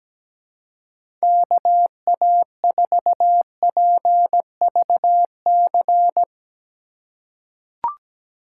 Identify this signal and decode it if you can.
Morse code, heard as a keyed tone, about 17 wpm, 700 Hz. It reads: KA4PVC
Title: USA Call Signs - 6 Characters - 4x Character Spacing - 17wpm